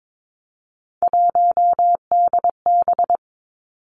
Morse code, heard as a keyed tone, about 22 wpm, 700 Hz. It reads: 1DB
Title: Three Challenging Letter-Number Combinations 22wpm